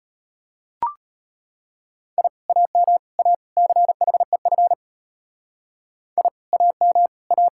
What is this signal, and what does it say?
Morse code, heard as a keyed tone, about 33 wpm, 700 Hz. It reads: I AM A CHEF I AM A